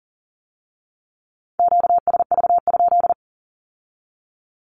Morse code, heard as a keyed tone, about 42 wpm, 700 Hz. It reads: QSV?
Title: Q-Codes with Spoken Meaning - 40wpm